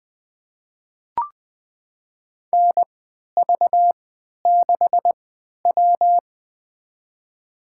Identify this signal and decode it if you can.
Morse code, heard as a keyed tone, about 20 wpm, spaced as for 11 wpm, 700 Hz. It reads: NV6W
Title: USA Call Signs - 4 Characters - 3x Character Spacing - 20wpm